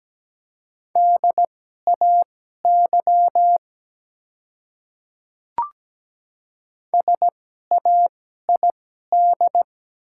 Morse code, heard as a keyed tone, about 17 wpm, spaced as for 12 wpm, 700 Hz. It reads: DAY SAID
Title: 2x Character Spacing - Top 200 Words - 17wpm